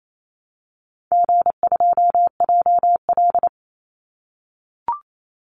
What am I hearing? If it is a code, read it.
Morse code, heard as a keyed tone, about 28 wpm, 700 Hz. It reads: G2JL
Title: Call Signs - 28wpm